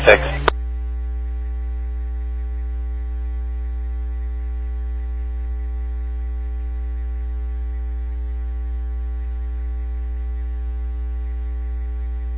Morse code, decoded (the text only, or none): none